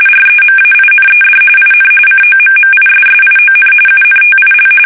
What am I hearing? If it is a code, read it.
RTTY, about 45.45 baud. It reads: REMAINED IN COMMISSION UNTIL
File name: fsk850 / battleshipnj